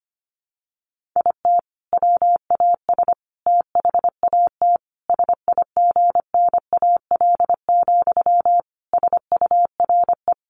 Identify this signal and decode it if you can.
Morse code, heard as a keyed tone, about 25 wpm, 700 Hz. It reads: IT WAS THAT SIGNAL, SURE
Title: Sentences from Top 1400 Words 25wpm